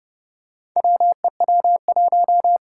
Morse code, heard as a keyed tone, about 30 wpm, 700 Hz. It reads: WEW1
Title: Four Challenging Letter-Number Combinations 30wpm